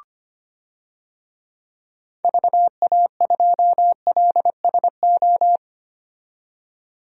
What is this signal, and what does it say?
Morse code, heard as a keyed tone, about 25 wpm, 700 Hz. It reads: VA2LSO
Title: DXCC-100 Call Signs - 6 Characters - 25wpm